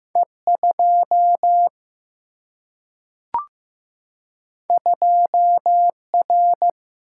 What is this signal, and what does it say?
Morse code, heard as a keyed tone, about 15 wpm, 700 Hz. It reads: E2 2R
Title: Single-Letter Single-Number 15wpm